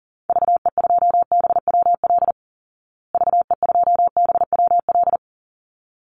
Morse code, heard as a keyed tone, about 40 wpm, 700 Hz. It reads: VE2BWL VE2BWL
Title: Call Signs Repeated 3x - 40wpm